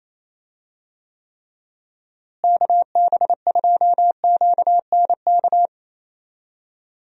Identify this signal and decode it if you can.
Morse code, heard as a keyed tone, about 28 wpm, 700 Hz. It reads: KB2QNK
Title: USA Call Signs - 6 Characters - Repeated 3x - 28wpm